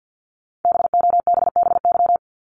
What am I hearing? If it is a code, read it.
Morse code, heard as a keyed tone, about 50 wpm, 690 Hz. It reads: BOBBY
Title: Multi-letter Phonemes Repeated 3x 50wpm